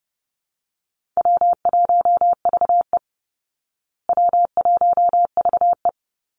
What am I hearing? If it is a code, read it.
Morse code, heard as a keyed tone, about 30 wpm, 700 Hz. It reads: W1VE W1VE
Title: Easy Call Signs Repeated 2x - 30wpm